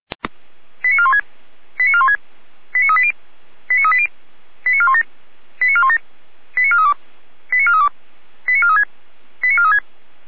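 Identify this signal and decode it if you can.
Five-tone selective call (ZVEI), heard as quick runs of five tones, tones of about 70 ms. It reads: 89416 89416 89380 89380 89417 89417 89432 89432 89436 89436
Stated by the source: SDR-received ZVEI five-tone selective call, code 89403.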